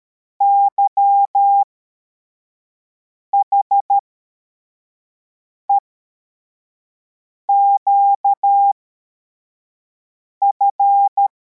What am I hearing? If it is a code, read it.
Morse code, heard as a keyed tone, about 13 wpm, spaced as for 4 wpm, 800 Hz. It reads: YHEQF